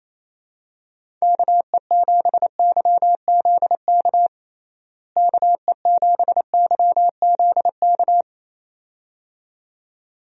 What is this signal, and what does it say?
Morse code, heard as a keyed tone, about 28 wpm, 700 Hz. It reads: KE7YZK KE7YZK
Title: USA Call Signs - 6 Characters - Repeated 2x - 28wpm